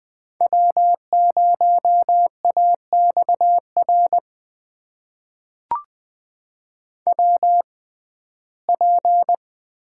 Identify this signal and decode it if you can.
Morse code, heard as a keyed tone, about 20 wpm, spaced as for 14 wpm, 700 Hz. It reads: W0AXR WP
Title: USA Call Signs - 5 Characters - 6x Character Spacing - 20wpm